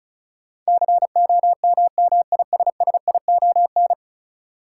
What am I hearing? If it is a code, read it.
Morse code, heard as a keyed tone, about 34 wpm, 700 Hz. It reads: COMMISSION
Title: Sentences from Top 1400 Words - New Word Drill - 35wpm